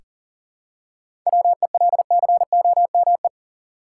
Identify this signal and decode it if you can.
Morse code, heard as a keyed tone, about 40 wpm, 700 Hz. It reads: WELCOME
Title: Seven Letter Words - 40wpm